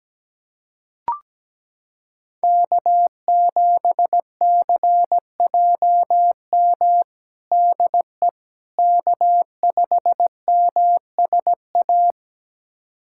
Morse code, heard as a keyed tone, about 17 wpm, 700 Hz. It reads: K7CJM DE K5MSA
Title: USA Callsign DE Callsign - 5 Characters - 17wpm